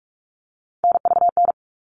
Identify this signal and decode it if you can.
Morse code, heard as a keyed tone, about 44 wpm, 700 Hz. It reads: NVN
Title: All Cut Numbers - 3-Digit 45wpm